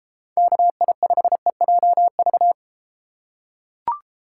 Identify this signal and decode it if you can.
Morse code, heard as a keyed tone, about 31 wpm, 700 Hz. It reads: KI5EJV